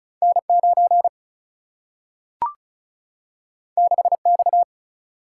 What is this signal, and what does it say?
Morse code, heard as a keyed tone, about 36 wpm, 700 Hz. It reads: N9 6X
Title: Single-Letter Single-Number 35wpm